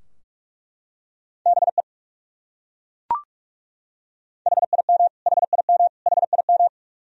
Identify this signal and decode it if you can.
Morse code, heard as a keyed tone, about 41 wpm, 700 Hz. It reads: BE HIM HIM HIM
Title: Top 100 Words Repeated 45wpm